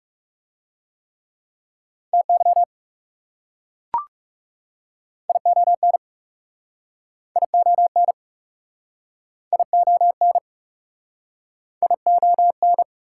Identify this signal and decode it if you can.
Morse code, heard as a keyed tone, about 40 wpm, 700 Hz. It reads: TY ION ION ION ION